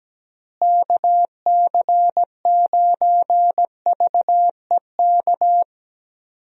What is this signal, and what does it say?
Morse code, heard as a keyed tone, about 17 wpm, 700 Hz. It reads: KC9VEK
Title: USA Call Signs - 6 Characters - Repeated 3x - 17wpm